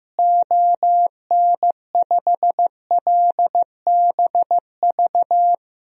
Morse code, heard as a keyed tone, about 15 wpm, 700 Hz. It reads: ON5LBV
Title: DXCC-100 Call Signs - 6 Characters - Repeated 2x - 15wpm